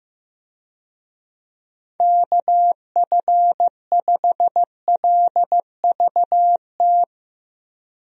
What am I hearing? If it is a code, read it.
Morse code, heard as a keyed tone, about 15 wpm, 700 Hz. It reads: KF5LVT
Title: USA Call Signs - 6 Characters - Repeated 3x - 15wpm